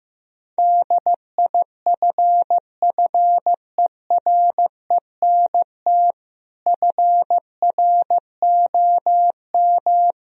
Morse code, heard as a keyed tone, about 15 wpm, 700 Hz. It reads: DIFFERENT FROM
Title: Sentences from Top 1000 Words - Review Part 3 - 15wpm